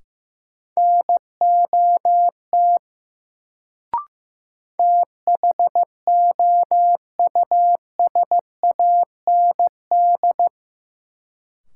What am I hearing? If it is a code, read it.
Morse code, heard as a keyed tone, about 15 wpm, 700 Hz. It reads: NOT THOUSAND
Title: Sentences from Top 600 Words - New Word Drill - 15wpm